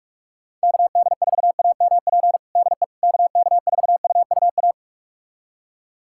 Morse code, heard as a keyed tone, about 46 wpm, 700 Hz. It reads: KD4AMP DE KK4UUA
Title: USA Callsign DE Callsign 45wpm